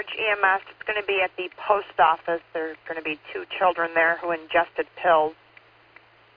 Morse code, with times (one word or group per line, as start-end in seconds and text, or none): none